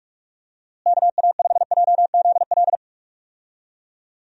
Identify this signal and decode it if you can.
Morse code, heard as a keyed tone, about 43 wpm, 700 Hz. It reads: KA5JZL
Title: USA Call Signs - 6 Characters - Repeated 2x - 45wpm